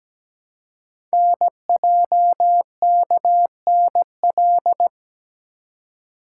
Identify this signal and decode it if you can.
Morse code, heard as a keyed tone, about 17 wpm, 700 Hz. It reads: NJKNL
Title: Code Groups - 5 Letter:Numbers - 17wpm